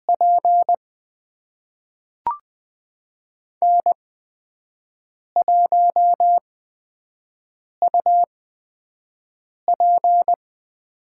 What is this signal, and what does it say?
Morse code, heard as a keyed tone, about 20 wpm, spaced as for 5 wpm, 700 Hz. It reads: P N1UP
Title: USA Call Signs - 8x Character Spacing - 20wpm